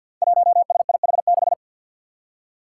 Morse code, heard as a keyed tone, about 52 wpm, 700 Hz. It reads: 1SIH6
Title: Five Challenging Letter-Number Combinations 50wpm